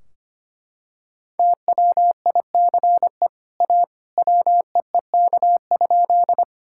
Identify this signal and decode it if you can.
Morse code, heard as a keyed tone, about 25 wpm, 700 Hz. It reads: TWICE A WEEK?
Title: Sentences from Top 1600 Words 25wpm